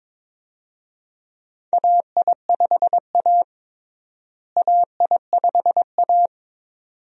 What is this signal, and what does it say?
Morse code, heard as a keyed tone, about 22 wpm, 700 Hz. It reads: AI5A AI5A
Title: USA Call Signs - 4 Characters - Repeated 2x - 22wpm